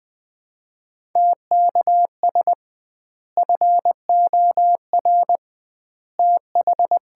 Morse code, heard as a keyed tone, about 20 wpm, 700 Hz. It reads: TKS FOR TH
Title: Common QSO Phrases - 2x Word-Spacing - 20wpm